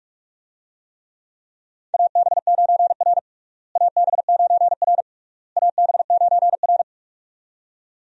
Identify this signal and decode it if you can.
Morse code, heard as a keyed tone, about 46 wpm, 700 Hz. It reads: AB9R AB9R AB9R
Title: USA Call Signs - 4 Characters - Repeated 3x - 45wpm